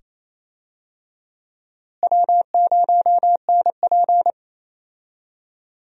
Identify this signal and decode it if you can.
Morse code, heard as a keyed tone, about 28 wpm, 700 Hz. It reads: W0NP
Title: USA Call Signs - 4 Characters - Repeated 2x - 28wpm